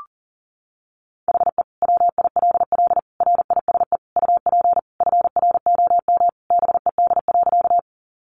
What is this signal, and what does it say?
Morse code, heard as a keyed tone, about 37 wpm, 700 Hz. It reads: HE WILL RISE UP FROM BED.
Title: Sentences from Top 900 Words 40wpm